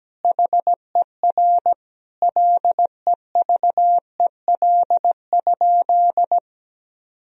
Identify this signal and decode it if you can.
Morse code, heard as a keyed tone, about 17 wpm, 700 Hz. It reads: HER LEVEL?
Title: Sentences from Top 1000 Words - Review Part 4 - 17wpm